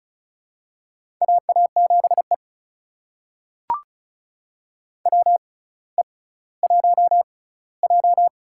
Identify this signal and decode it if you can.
Morse code, heard as a keyed tone, about 34 wpm, spaced as for 18 wpm, 700 Hz. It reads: AA7E WE1J
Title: USA Call Signs - 4 Characters - 6x Character Spacing - 35wpm